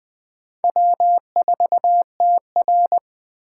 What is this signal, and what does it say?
Morse code, heard as a keyed tone, about 20 wpm, 700 Hz. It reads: W4TR
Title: USA Call Signs - 4 Characters - 8x Character Spacing - 20wpm